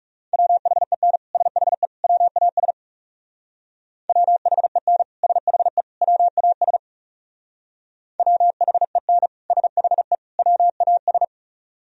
Morse code, heard as a keyed tone, about 39 wpm, 700 Hz. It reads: WHEN SHE WAS WHEN SHE WAS WHEN SHE WAS